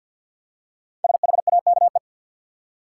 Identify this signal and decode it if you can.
Morse code, heard as a keyed tone, about 50 wpm, 700 Hz. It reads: SHAKE
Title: Five Letter Words - 50wpm